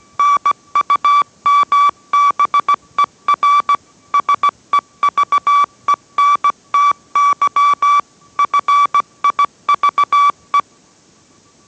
Morse code, heard as a keyed tone, about 17 wpm, 1150 Hz. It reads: NUMBER SEVENTY FIVE